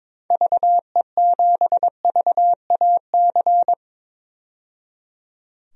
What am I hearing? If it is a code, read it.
Morse code, heard as a keyed tone, about 22 wpm, 700 Hz. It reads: VE7VAC
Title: DXCC-100 Call Signs - 6 Characters - 22wpm